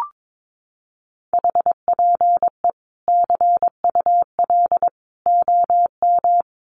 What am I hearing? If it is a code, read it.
Morse code, heard as a keyed tone, about 22 wpm, 700 Hz. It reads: HPE CUL OM
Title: Common QSO Phrases 22wpm